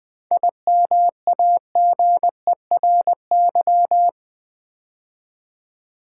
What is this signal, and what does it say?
Morse code, heard as a keyed tone, about 20 wpm, 700 Hz. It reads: IMAGERY
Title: 7 Letter Words - First 3 Letters Given - 20wpm